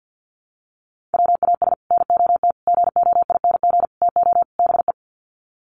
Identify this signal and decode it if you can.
Morse code, heard as a keyed tone, about 50 wpm, 700 Hz. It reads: WAS NOT GOING TO BE
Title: Sets of 5 Words Encore 50wpm